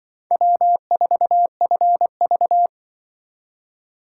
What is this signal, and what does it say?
Morse code, heard as a keyed tone, about 25 wpm, 700 Hz. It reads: W4FV
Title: Speed Racing - USA 1x2 Call Signs 20wpm